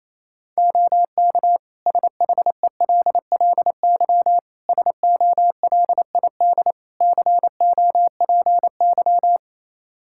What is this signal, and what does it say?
Morse code, heard as a keyed tone, about 28 wpm, 700 Hz. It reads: OK SHELLY SOLID COPY
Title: Practice QSO - Exchange 3 - More Info - 28wpm